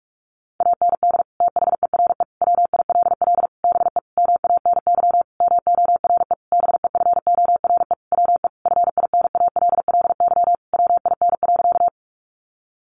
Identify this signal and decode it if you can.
Morse code, heard as a keyed tone, about 44 wpm, 700 Hz. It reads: AND THERE WILL BE MANY MORE BEFORE WE FINALLY WIN.
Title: Sentences from Top 1500 Words 45wpm